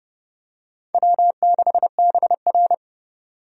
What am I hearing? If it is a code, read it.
Morse code, heard as a keyed tone, about 30 wpm, 700 Hz. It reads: W6BR
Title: USA Call Signs - 4 Characters - 30wpm